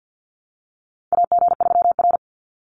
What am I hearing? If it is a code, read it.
Morse code, heard as a keyed tone, about 50 wpm, 700 Hz. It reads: AG3R
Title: Call Signs Repeated 3x - 50wpm